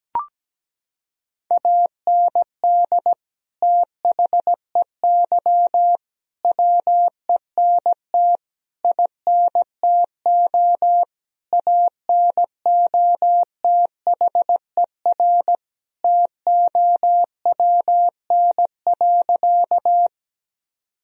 Morse code, heard as a keyed tone, about 17 wpm, 700 Hz. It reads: AND THEY WENT INTO ANOTHER TOWN.
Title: Sentences from Top 1000 Words - Review Part 2 - 17wpm